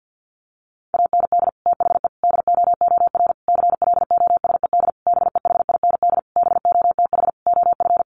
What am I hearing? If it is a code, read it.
Morse code, heard as a keyed tone, about 48 wpm, 700 Hz. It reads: AND THE DOOR CLOSED BEHIND BOTH OF
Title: Sentences from Top 1400 Words 50wpm